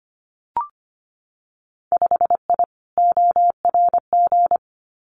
Morse code, heard as a keyed tone, about 25 wpm, 700 Hz. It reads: 5I ORG
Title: Winter Field Day Exchanges Encore 25wpm